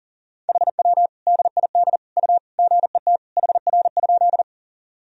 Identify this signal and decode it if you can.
Morse code, heard as a keyed tone, about 37 wpm, 700 Hz. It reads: HW DID U GET HR?
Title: Sentences from Top 200 Words and Abbreviations 40wpm